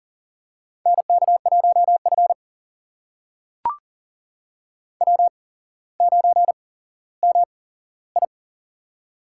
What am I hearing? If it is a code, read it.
Morse code, heard as a keyed tone, about 40 wpm, spaced as for 17 wpm, 700 Hz. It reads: NK1F W9MI
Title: USA Call Signs - 4 Characters - 8x Character Spacing - 40wpm